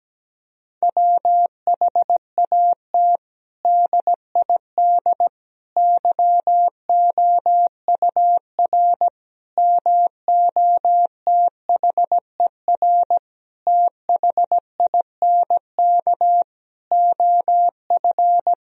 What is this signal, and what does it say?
Morse code, heard as a keyed tone, about 17 wpm, 700 Hz. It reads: WHAT DID YOUR MOTHER THINK OF
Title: Sentences from Top 1000 Words - Review Part 4 - 17wpm